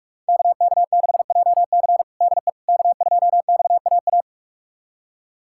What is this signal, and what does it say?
Morse code, heard as a keyed tone, about 44 wpm, 700 Hz. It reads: KK6JC DE K2XAA
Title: USA Callsign DE Callsign - 5 Characters - 45wpm